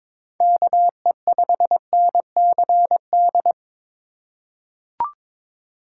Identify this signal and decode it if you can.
Morse code, heard as a keyed tone, about 22 wpm, 700 Hz. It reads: KE5NCD
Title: USA Call Signs - 6 Characters - 8x Character Spacing - 22wpm